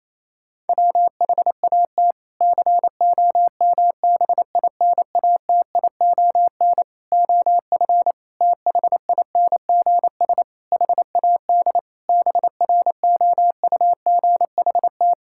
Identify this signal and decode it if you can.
Morse code, heard as a keyed tone, about 27 wpm, 700 Hz. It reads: WHAT COMBINATION OF THINGS HAD BROUGHT
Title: Sentences from Top 1000 Words - Review Part 6 - 28wpm